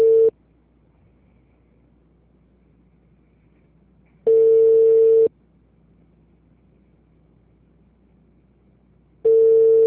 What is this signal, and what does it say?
Ringback tone (China, 450 Hz).
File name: out- -2007- - - .395